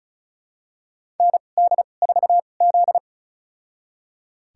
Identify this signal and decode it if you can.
Morse code, heard as a keyed tone, about 36 wpm, spaced as for 25 wpm, 700 Hz. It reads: ND4Z
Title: USA Call Signs - 4 Characters - 2x Character Spacing - 35wpm